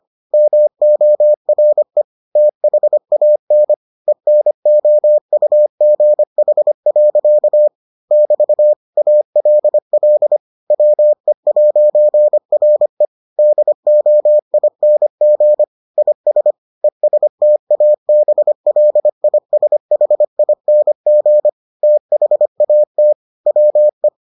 Morse code, heard as a keyed tone, about 25 wpm, 600 Hz. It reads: MORE THAN ENOUGH. <BT> ALL WE'RE DOING IS ESTABLISHING THAT WE